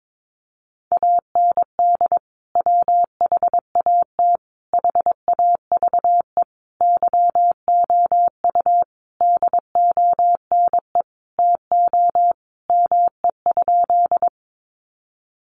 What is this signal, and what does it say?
Morse code, heard as a keyed tone, about 22 wpm, 700 Hz. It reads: AND WHAT HAVE YOU DONE TO ME?
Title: Sentences from Top 500 Words 22wpm